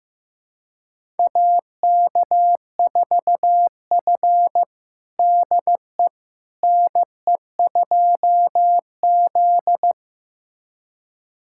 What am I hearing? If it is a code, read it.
Morse code, heard as a keyed tone, about 15 wpm, 700 Hz. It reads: AK4F DE NE2Z
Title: USA Callsign DE Callsign - 4 Characters - 15wpm